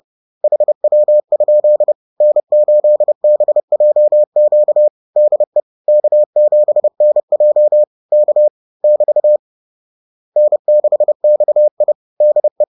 Morse code, heard as a keyed tone, about 30 wpm, 600 Hz. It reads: HW? N8BJQ DE K7NJ K <BT> N6XI DE
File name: 30wpm-4